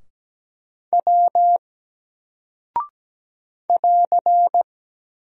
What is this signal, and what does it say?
Morse code, heard as a keyed tone, about 17 wpm, 700 Hz. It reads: W <AR>
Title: Single Character (Letter:Number:Punctuation:Prosign) 17wpm